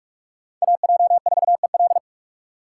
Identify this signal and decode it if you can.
Morse code, heard as a keyed tone, about 42 wpm, 700 Hz. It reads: AJ4EL
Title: USA Call Signs - 5 Characters - 6x Character Spacing - 45wpm